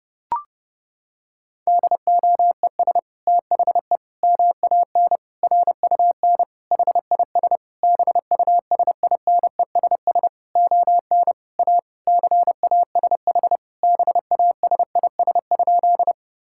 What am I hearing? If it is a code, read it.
Morse code, heard as a keyed tone, about 30 wpm, 700 Hz. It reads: DOES THE MAN RUN HIS BUSINESS ON A CASH BASIS?
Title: Sentences from Top 1000 Words - Review Part 5 - 30wpm